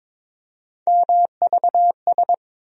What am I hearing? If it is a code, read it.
Morse code, heard as a keyed tone, about 22 wpm, 700 Hz. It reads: MVS
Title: USA Call Sign Postfixes - 3 Letters 22wpm